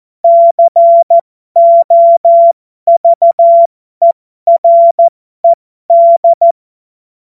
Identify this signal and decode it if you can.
Morse code, heard as a keyed tone, about 14 wpm, spaced as for 12 wpm, 680 Hz. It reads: COVERED
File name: Morsekurs / Witze